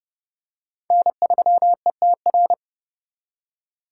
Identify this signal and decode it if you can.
Morse code, heard as a keyed tone, about 26 wpm, 700 Hz. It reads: N3ETR